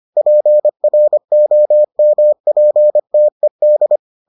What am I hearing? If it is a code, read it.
Morse code, heard as a keyed tone, about 25 wpm, 600 Hz. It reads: PROMPTED